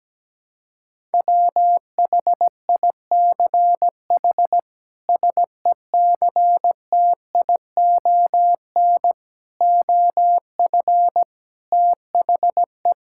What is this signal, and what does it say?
Morse code, heard as a keyed tone, about 17 wpm, 700 Hz. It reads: WHICH SECTION OF THE
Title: Sentences from Top 1000 Words - Review Part 5 - 17wpm